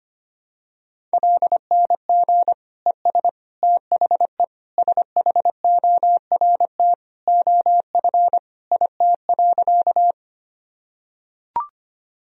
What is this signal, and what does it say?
Morse code, heard as a keyed tone, about 25 wpm, 700 Hz. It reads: LNG ES THE SHORT OF IT.
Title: Sentences from Top 400 Words and Abbreviations 25wpm